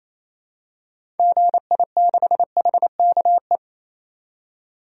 Morse code, heard as a keyed tone, about 28 wpm, 700 Hz. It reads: GI6HKE
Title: DXCC-100 Call Signs - 6 Characters - 28wpm